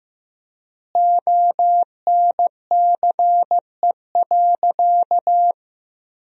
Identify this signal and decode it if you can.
Morse code, heard as a keyed tone, about 15 wpm, 700 Hz. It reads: ONCE.
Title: 3x Word-Spacing - Sentences from Top 500 Words - 15wpm